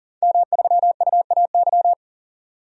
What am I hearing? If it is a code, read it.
Morse code, heard as a keyed tone, about 40 wpm, 700 Hz. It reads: M3UAY
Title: DXCC-100 Call Signs - 5 Characters - 40wpm